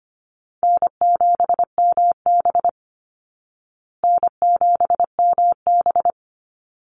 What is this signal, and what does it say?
Morse code, heard as a keyed tone, about 25 wpm, 700 Hz. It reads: N7MB N7MB
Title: Easy Call Signs Repeated 2x - 25wpm